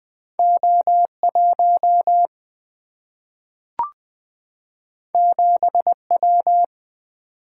Morse code, heard as a keyed tone, about 20 wpm, 700 Hz. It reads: O 1 7 W
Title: Single-Letter Single-Number 20wpm